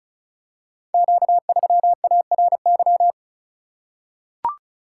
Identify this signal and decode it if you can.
Morse code, heard as a keyed tone, about 36 wpm, 700 Hz. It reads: Q3ARY
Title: Code Groups - 5 Letter:Numbers - 35wpm